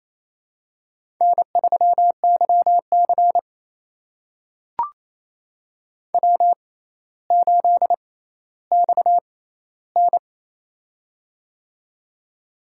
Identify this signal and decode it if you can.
Morse code, heard as a keyed tone, about 28 wpm, spaced as for 14 wpm, 700 Hz. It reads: N3YC W8XN
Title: USA Call Signs - 4 Characters - 6x Character Spacing - 28wpm